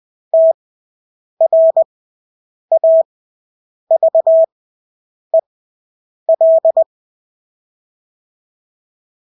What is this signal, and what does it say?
Morse code, heard as a keyed tone, about 20 wpm, spaced as for 8 wpm, 650 Hz. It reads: TRAVEL
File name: Engelske ord 20/8 wpm